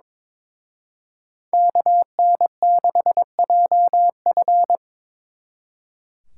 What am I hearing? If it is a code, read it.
Morse code, heard as a keyed tone, about 26 wpm, 700 Hz. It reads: KN6JF
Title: Speed Racing - USA 2x2 Call Signs 22wpm